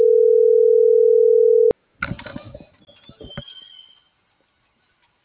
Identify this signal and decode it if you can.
Ringback tone (North America), 440 + 480 Hz.